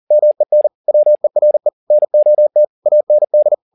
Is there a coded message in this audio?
Yes: Morse code from 0.1 to 3.5 s, heard as a keyed tone, about 40 wpm, 600 Hz. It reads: MEN WERE NOT AND